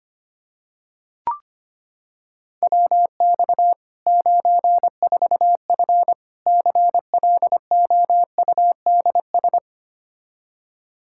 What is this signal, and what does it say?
Morse code, heard as a keyed tone, about 25 wpm, 700 Hz. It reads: WX 94F CLOUDS
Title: Practice QSO - Exchange 2 - Info - 25wpm